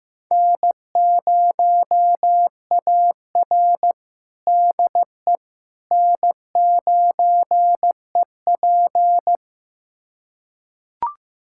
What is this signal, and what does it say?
Morse code, heard as a keyed tone, about 15 wpm, 700 Hz. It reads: N0AR DE N9EP
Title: USA Callsign DE Callsign - 4 Characters - 15wpm